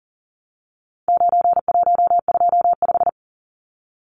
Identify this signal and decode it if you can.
Morse code, heard as a keyed tone, about 40 wpm, 700 Hz. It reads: 9125
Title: Numbers - 4-Digit 40wpm